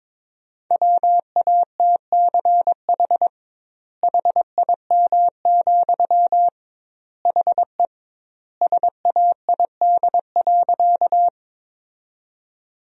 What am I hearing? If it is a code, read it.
Morse code, heard as a keyed tone, about 22 wpm, 700 Hz. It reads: WATCH HIM, HE SAID.
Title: 2x Word-Spacing - Sentences from Top 300 Words - 22wpm